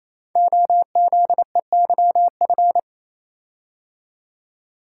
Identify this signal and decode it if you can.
Morse code, heard as a keyed tone, about 27 wpm, 700 Hz. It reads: OZEYF
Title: Code Groups - 5 Letter:Numbers - 28wpm